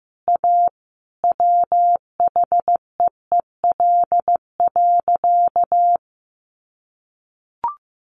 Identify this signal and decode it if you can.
Morse code, heard as a keyed tone, about 14 wpm, 700 Hz. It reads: A WHEEL.
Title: Sentences from Top 500 Words 15wpm